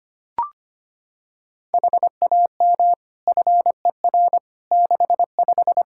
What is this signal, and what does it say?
Morse code, heard as a keyed tone, about 25 wpm, 700 Hz. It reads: HAM FER 65
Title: K1USN SST QSO Practice 25wpm